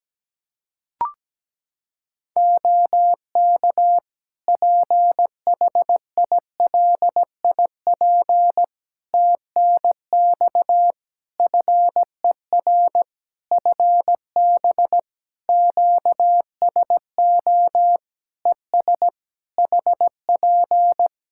Morse code, heard as a keyed tone, about 17 wpm, 700 Hz. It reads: OK PHILIP TNX FER FB QSO ES HP
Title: Practice QSO - Exchange 4 - Ending - 17wpm